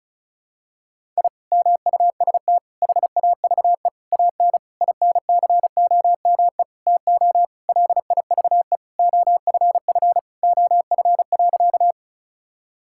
Morse code, heard as a keyed tone, about 34 wpm, 700 Hz. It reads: I MUST HAVE AN INCOME TO LIVE OFF OF.
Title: Sentences from Top 1000 Words - Review Part 5 - 35wpm